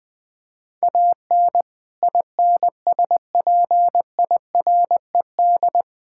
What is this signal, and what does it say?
Morse code, heard as a keyed tone, about 20 wpm, 700 Hz. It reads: AN INSPIRED
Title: Pangram Challenge v2 - 20wpm